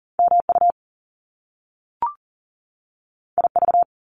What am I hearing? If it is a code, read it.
Morse code, heard as a keyed tone, about 42 wpm, 700 Hz. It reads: MU IV
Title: Top Two Letter Combinations in Words 40wpm